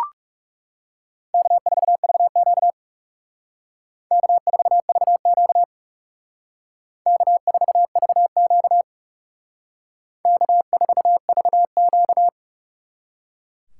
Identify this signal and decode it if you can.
Morse code, heard as a keyed tone, about 34 wpm, 700 Hz. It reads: K4VQ K4VQ K4VQ K4VQ
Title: Speed Racing - USA 1x2 Call Signs 30wpm